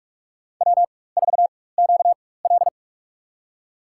Morse code, heard as a keyed tone, about 46 wpm, spaced as for 21 wpm, 700 Hz. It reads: W4QL
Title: USA Call Signs - 4 Characters - 4x Character Spacing - 45wpm